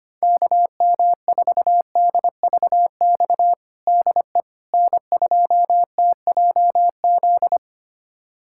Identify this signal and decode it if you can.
Morse code, heard as a keyed tone, about 25 wpm, 700 Hz. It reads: KM4DVX DE N2TJZ
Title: USA Callsign DE Callsign 25wpm